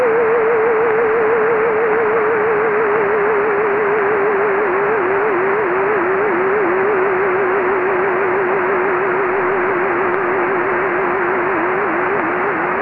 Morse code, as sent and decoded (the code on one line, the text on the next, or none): none